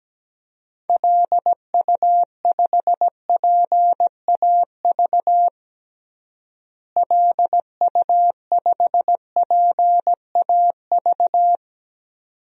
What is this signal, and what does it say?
Morse code, heard as a keyed tone, about 17 wpm, 700 Hz. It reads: LU5PAV LU5PAV
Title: DXCC-100 Call Signs - 6 Characters - Repeated 2x - 17wpm